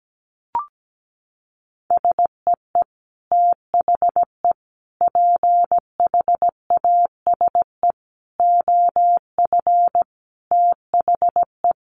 Morse code, heard as a keyed tone, about 17 wpm, 700 Hz. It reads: SEE THE PHASE OF THE
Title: Sentences from Top 1200 Words 17wpm